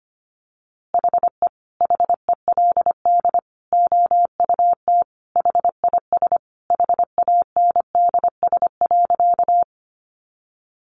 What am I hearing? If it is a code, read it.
Morse code, heard as a keyed tone, about 25 wpm, 700 Hz. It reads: HE HELD OUT HIS HANDS.
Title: A Sound Of Thunder 25wpm